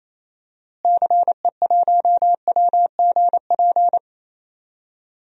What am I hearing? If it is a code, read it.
Morse code, heard as a keyed tone, about 28 wpm, 700 Hz. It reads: CE1WGP
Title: DXCC-100 Call Signs - 6 Characters - Repeated 3x - 28wpm